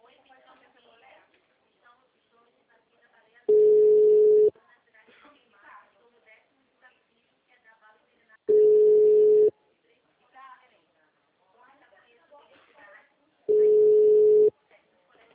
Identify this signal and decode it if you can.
Ringback tone (Europe, 425 Hz), with speech.
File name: out- -2021- - - .51676